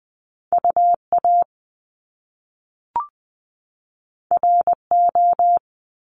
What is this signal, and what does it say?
Morse code, heard as a keyed tone, about 20 wpm, 700 Hz. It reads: UA RO
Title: Top Two Letter Combinations in Words 20wpm